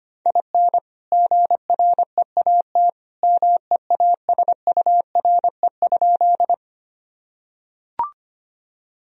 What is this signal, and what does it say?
Morse code, heard as a keyed tone, about 25 wpm, 700 Hz. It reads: IN GREAT MEASURE?
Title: Sentences from Top 1000 Words - Review Part 2 - 25wpm